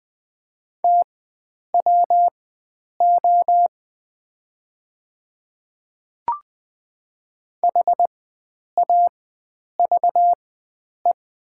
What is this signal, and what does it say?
Morse code, heard as a keyed tone, about 20 wpm, spaced as for 9 wpm, 700 Hz. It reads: TWO HAVE
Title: 4x Character Spacing - Top 100 Words - 20wpm